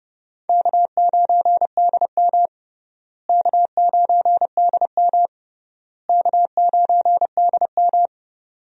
Morse code, heard as a keyed tone, about 29 wpm, 700 Hz. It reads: K9DM K9DM K9DM
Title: USA Call Signs - 4 Characters - Repeated 3x - 30wpm